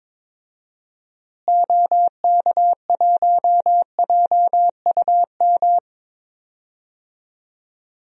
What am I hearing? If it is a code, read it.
Morse code, heard as a keyed tone, about 22 wpm, 700 Hz. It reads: OK1JUM
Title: DXCC-100 Call Signs - 6 Characters - Repeated 2x - 22wpm